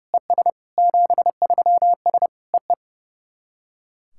Morse code, heard as a keyed tone, about 30 wpm, 690 Hz. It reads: ES 73S EE
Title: SOTA QSO Practice 30wpm